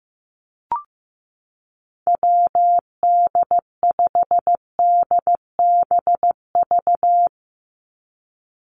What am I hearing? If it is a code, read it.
Morse code, heard as a keyed tone, about 15 wpm, 700 Hz. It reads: WD5DBV
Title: Call Signs Repeated 2x - 15wpm